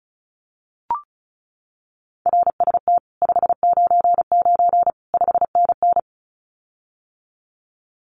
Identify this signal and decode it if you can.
Morse code, heard as a keyed tone, about 34 wpm, 700 Hz. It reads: RST 599 5NN
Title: Common QSO Phrases Encore 35wpm